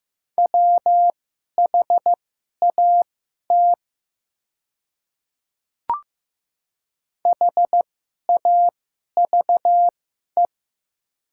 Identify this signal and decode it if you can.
Morse code, heard as a keyed tone, about 15 wpm, spaced as for 11 wpm, 700 Hz. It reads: WHAT HAVE
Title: 2x Character Spacing - Top 100 Words - 15wpm